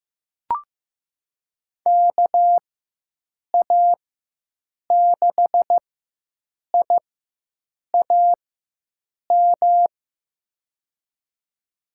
Morse code, heard as a keyed tone, about 15 wpm, spaced as for 7 wpm, 700 Hz. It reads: KA6IAM
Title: USA Call Signs - 6 Characters - 4x Character Spacing - 15wpm